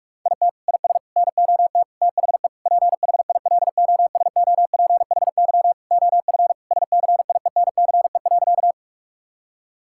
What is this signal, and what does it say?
Morse code, heard as a keyed tone, about 41 wpm, 700 Hz. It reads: IT IS NOT THE PHILOSOPHY OF SCIENCE.
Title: Sentences from Top 1000 Words - Review Part 6 - 45wpm